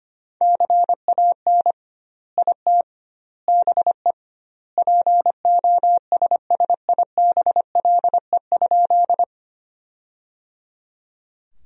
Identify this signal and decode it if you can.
Morse code, heard as a keyed tone, about 25 wpm, 700 Hz. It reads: CAN IT BE POSSIBLE?
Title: 2x Word-Spacing - Sentences from Top 500 Words - 25wpm